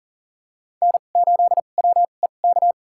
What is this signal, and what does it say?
Morse code, heard as a keyed tone, about 40 wpm, spaced as for 29 wpm, 700 Hz. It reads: N8WEK
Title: USA Call Signs - 5 Characters - 2x Character Spacing - 40wpm